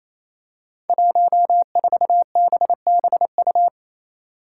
Morse code, heard as a keyed tone, about 28 wpm, 700 Hz. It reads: 14BBU
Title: Five Challenging Letter-Number Combinations 28wpm